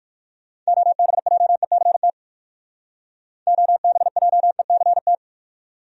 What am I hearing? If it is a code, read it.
Morse code, heard as a keyed tone, about 58 wpm, 700 Hz. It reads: OBJECT OBJECT